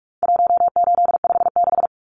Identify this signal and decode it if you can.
Morse code, heard as a keyed tone, about 44 wpm, 700 Hz. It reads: 1856
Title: Numbers - 4-Digit 45wpm